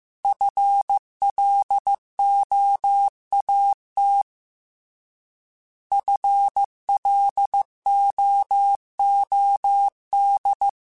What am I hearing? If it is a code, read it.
Morse code, heard as a keyed tone, about 15 wpm, 800 Hz. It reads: FLOAT FLOOD
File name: Alphabetical / Alphabetical mp3s / Fs @13